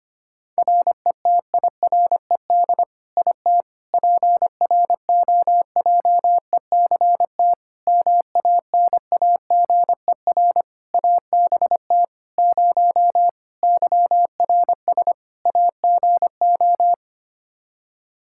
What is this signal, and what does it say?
Morse code, heard as a keyed tone, about 25 wpm, 700 Hz. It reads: RETIRED IT PROJECT MANAGER ABT 0 YRS AGO
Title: Practice QSO - Exchange 3 - More Info - 25wpm